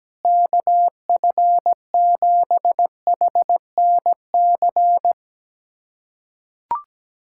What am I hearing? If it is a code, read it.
Morse code, heard as a keyed tone, about 17 wpm, 700 Hz. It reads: KF7HNC
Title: USA Callsign DE Callsign - 6 Characters - 17wpm